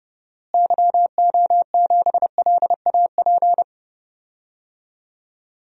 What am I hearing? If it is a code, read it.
Morse code, heard as a keyed tone, about 30 wpm, 700 Hz. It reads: YO7LAP
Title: DXCC-100 Call Signs - 6 Characters - Repeated 3x - 30wpm